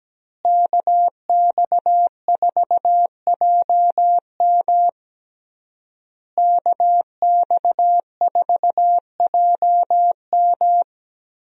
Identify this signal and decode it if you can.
Morse code, heard as a keyed tone, about 17 wpm, 700 Hz. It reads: KX4JM KX4JM
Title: USA Call Signs - 5 Characters - Repeated 3x - 17wpm